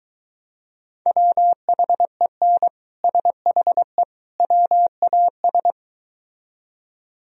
Morse code, heard as a keyed tone, about 23 wpm, 700 Hz. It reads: WHEN SHE WAS